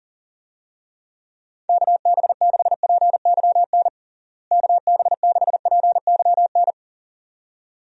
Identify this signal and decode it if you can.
Morse code, heard as a keyed tone, about 40 wpm, 700 Hz. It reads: KB6PYN KB6PYN
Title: USA Call Signs - 6 Characters - Repeated 2x - 40wpm